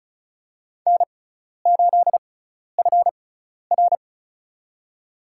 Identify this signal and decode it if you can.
Morse code, heard as a keyed tone, about 34 wpm, spaced as for 12 wpm, 700 Hz. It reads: N8FR
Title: USA Call Signs - 4 Characters - 6x Character Spacing - 35wpm